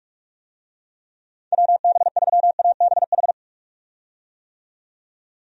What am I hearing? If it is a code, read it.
Morse code, heard as a keyed tone, about 42 wpm, 700 Hz. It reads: WB3ABH
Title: USA Call Signs - 6 Characters - 45wpm